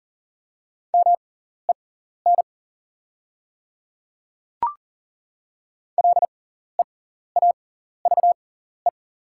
Morse code, heard as a keyed tone, about 40 wpm, spaced as for 14 wpm, 690 Hz. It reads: MEN LEAVE
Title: 6x Character Spacing - Top 500 Words - 40wpm